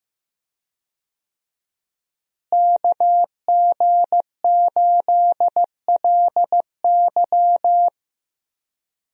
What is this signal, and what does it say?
Morse code, heard as a keyed tone, about 15 wpm, 700 Hz. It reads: KG8LY
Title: USA Call Signs - 5 Characters - Repeated 3x - 15wpm